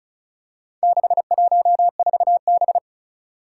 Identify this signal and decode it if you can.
Morse code, heard as a keyed tone, about 36 wpm, 700 Hz. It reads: 614B
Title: Four Challenging Letter-Number Combinations 35wpm